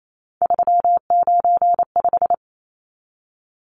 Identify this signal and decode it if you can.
Morse code, heard as a keyed tone, about 28 wpm, 700 Hz. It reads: 395
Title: Numbers - 3-Digit 28wpm